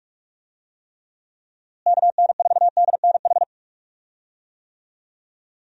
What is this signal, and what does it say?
Morse code, heard as a keyed tone, about 42 wpm, 700 Hz. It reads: KN4DNH
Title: USA Call Signs - 6 Characters - Repeated 3x - 45wpm